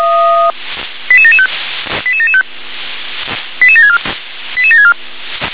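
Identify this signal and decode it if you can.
Five-tone selective call (ZVEI), heard as quick runs of five tones, tones of about 70 ms. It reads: 88775 88775 88764 88764
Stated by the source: ZVEI five-tone selective call 88786 received on SDR.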